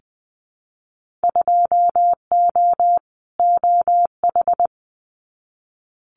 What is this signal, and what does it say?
Morse code, heard as a keyed tone, about 20 wpm, 700 Hz. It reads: 2O OH
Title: Winter Field Day Exchanges 20wpm